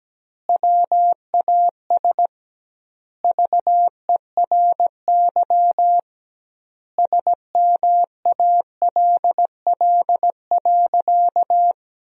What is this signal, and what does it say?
Morse code, heard as a keyed tone, about 17 wpm, 700 Hz. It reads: WAS VERY SMALL.
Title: 2x Word-Spacing - Sentences from Top 400 Words - 17wpm